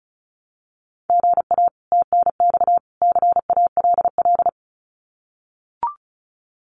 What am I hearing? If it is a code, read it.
Morse code, heard as a keyed tone, about 34 wpm, 700 Hz. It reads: GA TNX CALL
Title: Common QSO Phrases 35wpm